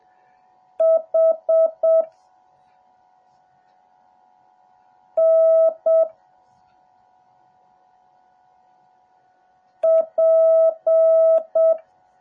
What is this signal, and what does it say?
Morse code, heard as a keyed tone, about 7 wpm, 650 Hz. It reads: H N P
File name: jh8cbh / 2021cw / cw08a